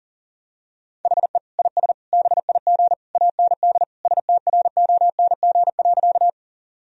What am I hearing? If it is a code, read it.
Morse code, heard as a keyed tone, about 40 wpm, 700 Hz. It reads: HE IS BIG AND STRONG.
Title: Sentences from Top 1000 Words - Review Part 3 - 40wpm